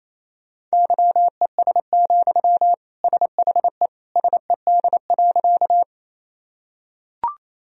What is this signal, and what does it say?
Morse code, heard as a keyed tone, about 28 wpm, 700 Hz. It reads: YES, SHE SED.
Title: Sentences from Top 500 Words and Abbreviations 28wpm